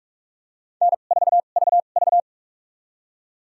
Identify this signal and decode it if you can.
Morse code, heard as a keyed tone, about 47 wpm, spaced as for 33 wpm, 700 Hz. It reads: N4VV
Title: USA Call Signs - 4 Characters - 2x Character Spacing - 45wpm